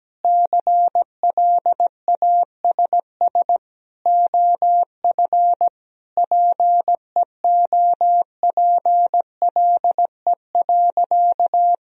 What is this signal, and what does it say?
Morse code, heard as a keyed tone, about 17 wpm, 700 Hz. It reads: CLASS OF PEOPLE.
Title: Sentences from Top 1000 Words - Review Part 2 - 17wpm